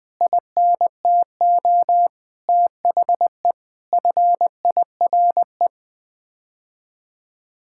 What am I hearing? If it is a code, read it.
Morse code, heard as a keyed tone, about 20 wpm, 700 Hz. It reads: INTO THE FIRE
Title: Sets of 9 Words - 20wpm